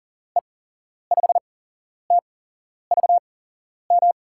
Morse code, heard as a keyed tone, about 40 wpm, spaced as for 11 wpm, 700 Hz. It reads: E5TVM
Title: USA Call Signs - 6 Characters - 8x Character Spacing - 40wpm